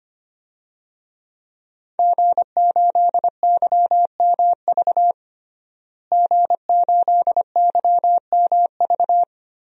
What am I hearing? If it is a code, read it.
Morse code, heard as a keyed tone, about 25 wpm, 700 Hz. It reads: G8YMV G8YMV
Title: DXCC-100 Call Signs - 5 Characters - Repeated 2x - 25wpm